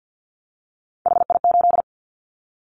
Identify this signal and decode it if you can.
Morse code, heard as a keyed tone, about 50 wpm, 700 Hz. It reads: HI8
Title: Callsign Prefixes for Top 100 DXCC - 50wpm